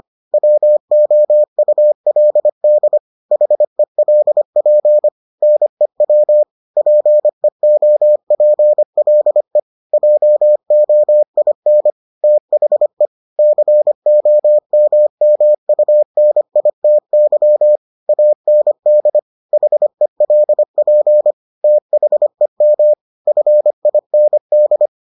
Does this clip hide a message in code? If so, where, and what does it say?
Yes: Morse code from 0.3 to 24.9 s, heard as a keyed tone, about 25 wpm, 600 Hz. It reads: WOULD HELP NEW PEOPLE JOIN THE COMMUNITY AND HELP THEM FIND